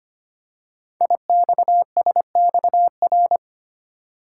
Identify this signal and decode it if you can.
Morse code, heard as a keyed tone, about 25 wpm, 700 Hz. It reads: IXSXR
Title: Code Groups - 5 Letter:Numbers - 25wpm